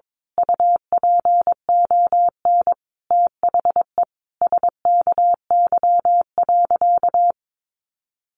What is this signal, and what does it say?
Morse code, heard as a keyed tone, about 22 wpm, 700 Hz. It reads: UPON THE SKY.
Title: A Sound Of Thunder 22wpm